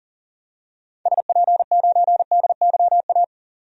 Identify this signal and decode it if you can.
Morse code, heard as a keyed tone, about 40 wpm, 700 Hz. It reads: SP9DYA
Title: DXCC-100 Call Signs - 6 Characters - 40wpm